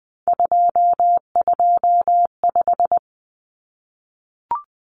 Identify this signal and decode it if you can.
Morse code, heard as a keyed tone, about 20 wpm, 700 Hz. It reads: 225
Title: Common Cut Numbers - 3-Digit 20wpm